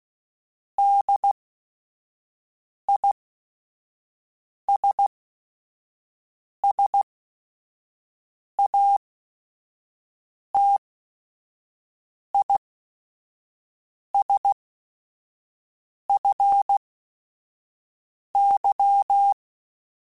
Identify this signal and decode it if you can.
Morse code, heard as a keyed tone, about 16 wpm, spaced as for 5 wpm, 800 Hz. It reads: DISSATISFY